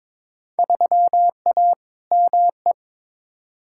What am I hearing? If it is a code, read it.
Morse code, heard as a keyed tone, about 22 wpm, 700 Hz. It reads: 3A ME
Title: Field Day Exchanges - 22wpm